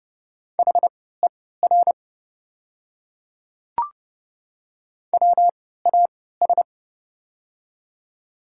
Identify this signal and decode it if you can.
Morse code, heard as a keyed tone, about 30 wpm, spaced as for 17 wpm, 700 Hz. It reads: HER WAS
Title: 3x Character Spacing - Top 100 Words - 30wpm